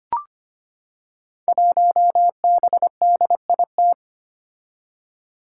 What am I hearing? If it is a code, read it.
Morse code, heard as a keyed tone, about 25 wpm, 700 Hz. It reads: 1BDIT
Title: Five Challenging Letter-Number Combinations 25wpm